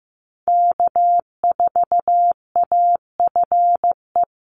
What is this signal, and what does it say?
Morse code, heard as a keyed tone, about 15 wpm, 700 Hz. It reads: K4AFE
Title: Call Signs Repeated 2x - 15wpm